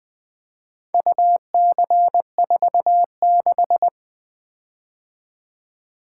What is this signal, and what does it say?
Morse code, heard as a keyed tone, about 21 wpm, 700 Hz. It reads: UC46
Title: Four Letter-Number Combinations - 20wpm